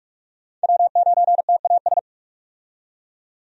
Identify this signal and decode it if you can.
Morse code, heard as a keyed tone, about 44 wpm, 700 Hz. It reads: W9TAS
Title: USA Call Signs - 5 Characters - 45wpm